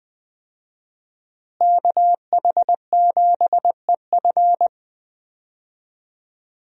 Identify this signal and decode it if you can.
Morse code, heard as a keyed tone, about 20 wpm, 700 Hz. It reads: KH7EF
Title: USA Call Signs - 5 Characters - Repeated 2x - 20wpm